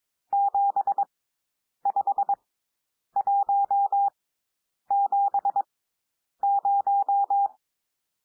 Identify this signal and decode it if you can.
Morse code, heard as a keyed tone, about 22 wpm, spaced as for 9 wpm, 800 Hz. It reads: 75170
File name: cwmp3 / N1 SOUND NUMBERS 22 wpm